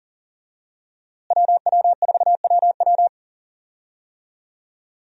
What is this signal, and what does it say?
Morse code, heard as a keyed tone, about 40 wpm, 700 Hz. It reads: WW4WW
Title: USA Call Signs - 5 Characters - Repeated 2x - 40wpm